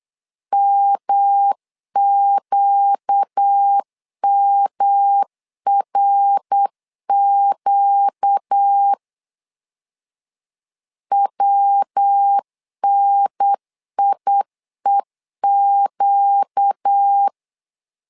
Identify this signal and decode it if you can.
Morse code, heard as a keyed tone, about 8 wpm, 800 Hz. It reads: MQMRQ WNIEQ
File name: 4 Q 8wpm